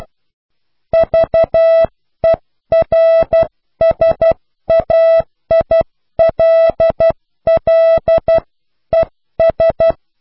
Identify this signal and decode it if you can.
Morse code, heard as a keyed tone, about 12 wpm, 660 Hz. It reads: VERSAILLES
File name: cw / pseudo worden(10)